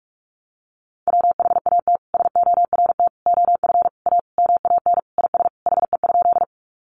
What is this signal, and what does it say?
Morse code, heard as a keyed tone, about 44 wpm, 700 Hz. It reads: WHAT SORT OF A MAN IS HE?
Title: Sentences from Top 1300 Words 45wpm